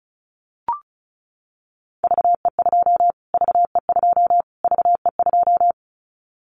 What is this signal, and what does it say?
Morse code, heard as a keyed tone, about 34 wpm, 700 Hz. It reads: VE2 VE2 VE2
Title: Callsign Prefixes for Top 25 DXCC - Repeated 3x - 35wpm